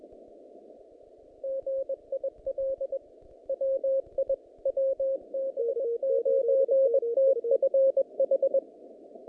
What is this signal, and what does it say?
Morse code, heard as a keyed tone, about 20 wpm, 540 Hz. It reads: GIL WI WB9TFH
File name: contests / naqp-cw-k5tr-op-k5pi- -2200